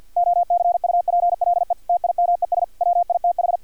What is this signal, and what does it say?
Morse code, heard as a keyed tone, about 50 wpm, 700 Hz. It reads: OCUPLE TIMES WITH
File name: archived / CRSnet